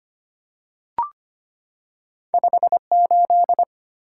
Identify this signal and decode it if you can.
Morse code, heard as a keyed tone, about 25 wpm, 700 Hz. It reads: 58
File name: Two Letter-Number Combinations - 25wpm